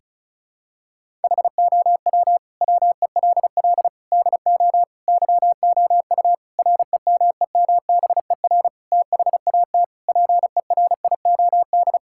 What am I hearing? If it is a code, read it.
Morse code, heard as a keyed tone, about 36 wpm, 700 Hz. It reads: HOW WELL DO YOU REMEMBER THAT PERIOD
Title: Sentences from Top 1000 Words - Review Part 5 - 35wpm